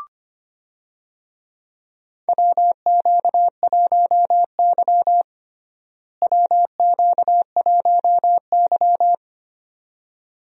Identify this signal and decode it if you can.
Morse code, heard as a keyed tone, about 25 wpm, 700 Hz. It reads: WQ1Y WQ1Y
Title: USA Call Signs - 4 Characters - Repeated 2x - 25wpm